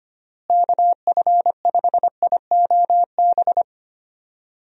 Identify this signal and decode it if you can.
Morse code, heard as a keyed tone, about 25 wpm, 700 Hz. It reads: KF5IOB
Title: USA Call Signs - 6 Characters - Repeated 2x - 25wpm